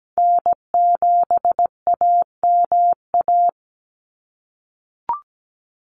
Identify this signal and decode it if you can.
Morse code, heard as a keyed tone, about 17 wpm, 700 Hz. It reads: N7AMA
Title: NA SOTA Chaser Callsigns 17wpm